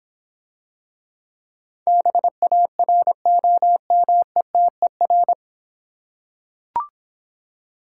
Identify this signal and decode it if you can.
Morse code, heard as a keyed tone, about 25 wpm, 700 Hz. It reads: BAROMETER
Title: Difficult Words - Speed-Racing with Hints - 20wpm